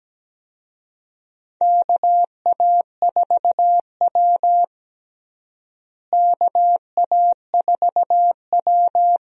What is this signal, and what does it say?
Morse code, heard as a keyed tone, about 17 wpm, 700 Hz. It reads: KA4W KA4W
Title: USA Call Signs - 4 Characters - Repeated 3x - 17wpm